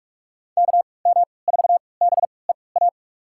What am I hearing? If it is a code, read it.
Morse code, heard as a keyed tone, about 44 wpm, spaced as for 25 wpm, 690 Hz. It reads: KM4BEA
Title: USA Call Signs - 6 Characters - 3x Character Spacing - 45wpm